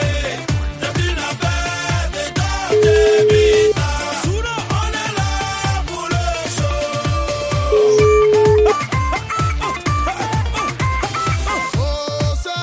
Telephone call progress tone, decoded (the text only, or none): ringback tone